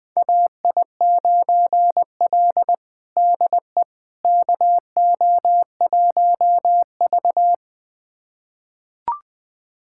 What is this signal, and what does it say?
Morse code, heard as a keyed tone, about 21 wpm, 700 Hz. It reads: AI9L DE KO1V
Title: USA Callsign DE Callsign - 4 Characters - 20wpm